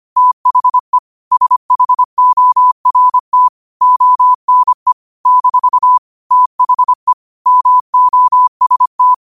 Morse code, heard as a keyed tone, about 25 wpm, 1000 Hz. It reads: THE SHORT ONE - THE MOST
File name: qotdfiles / 25wpm